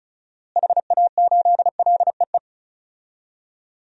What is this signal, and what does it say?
Morse code, heard as a keyed tone, about 34 wpm, 700 Hz. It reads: HA8LEE
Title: DXCC-100 Call Signs - 6 Characters - Repeated 2x - 35wpm